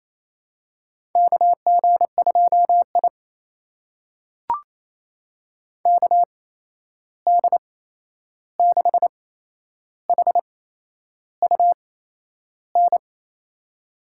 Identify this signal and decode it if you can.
Morse code, heard as a keyed tone, about 27 wpm, spaced as for 10 wpm, 700 Hz. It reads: KG2I KD6HUN
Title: USA Call Signs - 8x Character Spacing - 28wpm